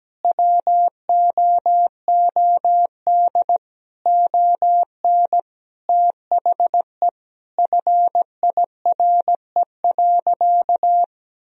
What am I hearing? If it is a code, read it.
Morse code, heard as a keyed tone, about 17 wpm, 700 Hz. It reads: WOOD ON THE FIRE.
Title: Sentences from Top 1000 Words - Review Part 2 - 17wpm